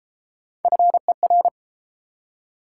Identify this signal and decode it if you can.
Morse code, heard as a keyed tone, about 30 wpm, 700 Hz. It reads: FER